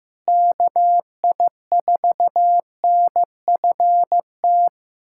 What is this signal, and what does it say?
Morse code, heard as a keyed tone, about 15 wpm, 700 Hz. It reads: KI4NFT
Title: USA Callsign DE Callsign - 6 Characters - 15wpm